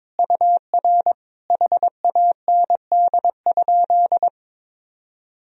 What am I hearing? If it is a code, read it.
Morse code, heard as a keyed tone, about 22 wpm, 700 Hz. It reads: UR HAND?
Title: Sentences from Top 200 Words and Abbreviations 22wpm